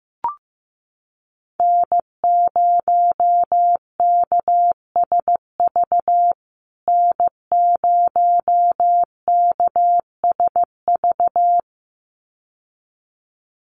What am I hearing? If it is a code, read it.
Morse code, heard as a keyed tone, about 15 wpm, 700 Hz. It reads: N0KSV N0KSV
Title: USA Call Signs Repeated 2x 15wpm